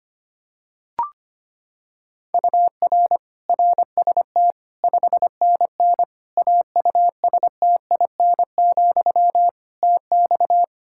Morse code, heard as a keyed tone, about 25 wpm, 700 Hz. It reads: UR RST 5NN AUSTIN, TX
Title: Practice QSO - Exchange 1 - Beginning - 25wpm